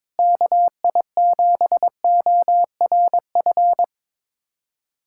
Morse code, heard as a keyed tone, about 22 wpm, 700 Hz. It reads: KI7ORF
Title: USA Callsign DE Callsign 22wpm